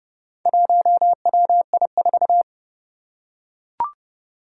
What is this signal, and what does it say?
Morse code, heard as a keyed tone, about 30 wpm, 700 Hz. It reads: 1WI4
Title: Four Challenging Letter-Number Combinations 30wpm